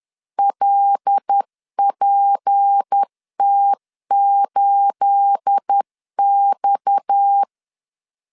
Morse code, heard as a keyed tone, about 10 wpm, 800 Hz. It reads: LPT8X
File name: LEZIONI / 9 286 10wpm 2735